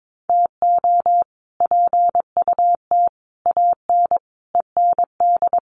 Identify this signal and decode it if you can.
Morse code, heard as a keyed tone, about 22 wpm, 700 Hz. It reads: TO PUT AN END
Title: Sets of 5 Words Encore2 22wpm